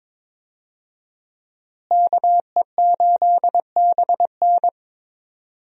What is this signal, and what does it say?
Morse code, heard as a keyed tone, about 21 wpm, 700 Hz. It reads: KE8BN
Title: USA Call Signs - 5 Characters - Repeated 3x - 22wpm